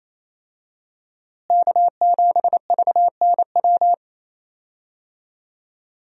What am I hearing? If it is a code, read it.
Morse code, heard as a keyed tone, about 28 wpm, 700 Hz. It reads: K7VNW
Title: USA Call Signs - 5 Characters - Repeated 2x - 28wpm